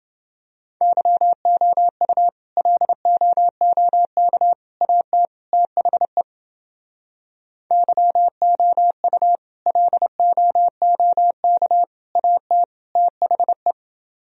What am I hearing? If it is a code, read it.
Morse code, heard as a keyed tone, about 29 wpm, 700 Hz. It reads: YOU LOOK AT THE YOU LOOK AT THE